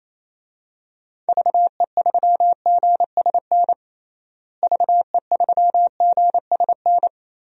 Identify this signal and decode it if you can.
Morse code, heard as a keyed tone, about 27 wpm, 700 Hz. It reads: VE3GSN VE3GSN
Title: DXCC-100 Call Signs - 6 Characters - Repeated 2x - 28wpm